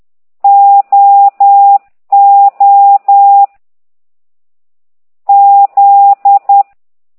Morse code, heard as a keyed tone, about 10 wpm, 800 Hz. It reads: O O Z
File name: ct2idw / exerc13